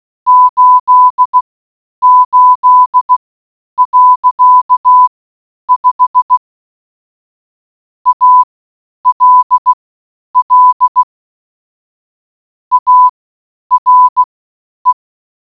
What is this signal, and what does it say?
Morse code, heard as a keyed tone, about 16 wpm, spaced as for 9 wpm, 1000 Hz. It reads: 88.5 ALL ARE